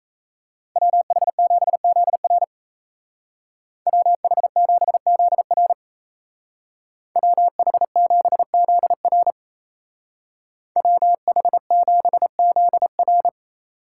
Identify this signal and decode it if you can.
Morse code, heard as a keyed tone, about 33 wpm, 700 Hz. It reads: WH7ZR WH7ZR WH7ZR WH7ZR
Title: Speed Racing - USA 2x2 Call Signs 28wpm